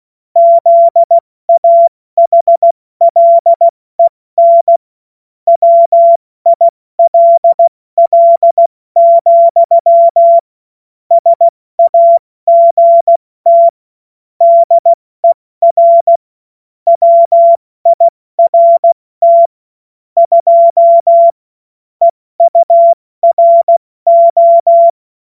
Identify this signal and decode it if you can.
Morse code, heard as a keyed tone, about 16 wpm, spaced as for 14 wpm, 680 Hz. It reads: ZAHLEN WILL, SAGT DER WIRT 2 EURO
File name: Morsekurs / Witze / Witz 11. Mai 2021, 70 F80